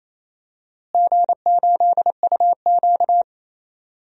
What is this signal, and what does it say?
Morse code, heard as a keyed tone, about 27 wpm, 700 Hz. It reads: G8UQ
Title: DXCC-100 Call Signs - 4 Characters - Repeated 2x - 28wpm